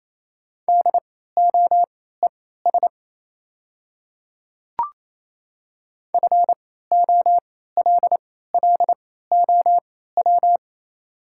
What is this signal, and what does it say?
Morse code, heard as a keyed tone, about 27 wpm, spaced as for 16 wpm, 690 Hz. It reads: DOES FOLLOW
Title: 3x Character Spacing - Top 200 Words - 28wpm